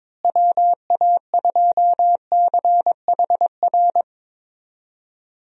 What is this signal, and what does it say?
Morse code, heard as a keyed tone, about 22 wpm, 700 Hz. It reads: WA2CHR
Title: USA Call Signs - 6 Characters - Repeated 2x - 22wpm